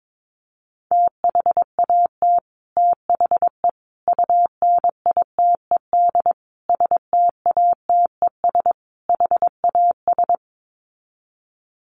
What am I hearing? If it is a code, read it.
Morse code, heard as a keyed tone, about 22 wpm, 700 Hz. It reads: THAT THE UNITED STATES HAS
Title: Sets of 5 Words Encore 22wpm